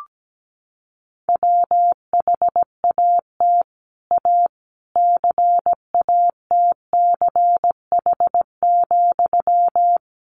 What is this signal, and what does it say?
Morse code, heard as a keyed tone, about 17 wpm, 700 Hz. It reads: WHAT A CATCH,
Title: Sentences from Top 1500 Words 17wpm